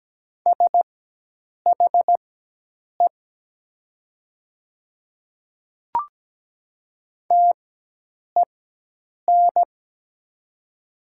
Morse code, heard as a keyed tone, about 17 wpm, spaced as for 8 wpm, 700 Hz. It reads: SHE TEN
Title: 4x Character Spacing - Top 500 Words - 17wpm